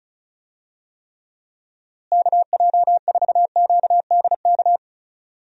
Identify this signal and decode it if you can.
Morse code, heard as a keyed tone, about 36 wpm, 700 Hz. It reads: KJ4QDK
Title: USA Call Signs - 6 Characters - Repeated 3x - 35wpm